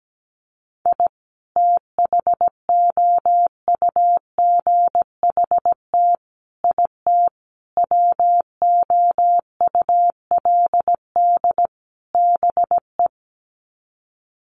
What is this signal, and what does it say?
Morse code, heard as a keyed tone, about 17 wpm, 700 Hz. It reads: I THOUGHT IT WOULD BE
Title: Sets of 5 Words Encore 17wpm